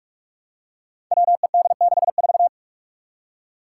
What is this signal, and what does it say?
Morse code, heard as a keyed tone, about 44 wpm, 700 Hz. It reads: WED64
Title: Five Challenging Letter-Number Combinations 45wpm